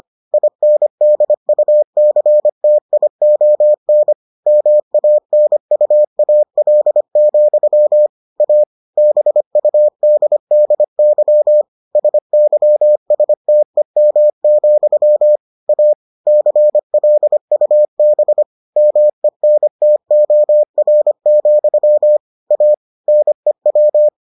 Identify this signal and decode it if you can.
Morse code, heard as a keyed tone, about 25 wpm, 600 Hz. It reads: INDUCTION MANUAL, A BUDDY SYSTEM, A CLUB MENTOR, A NEW